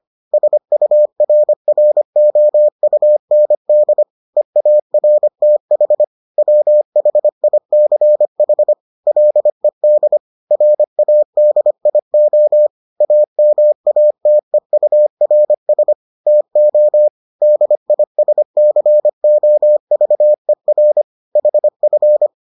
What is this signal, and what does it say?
Morse code, heard as a keyed tone, about 25 wpm, 600 Hz. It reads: SURROUND EARTH WHICH LED RADIO AMATEURS TO DISCOVER HF